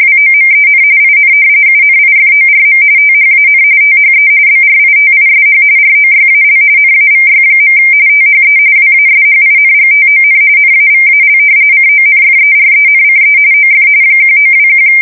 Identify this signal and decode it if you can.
RTTY, about 45.45 baud. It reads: WOMEN IN UNIFORM THAN THE BRITISH ROYAL NAVY. THE WASHINGTON NAVAL CONFERENCE OF 1921
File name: fsk170 / usnavy1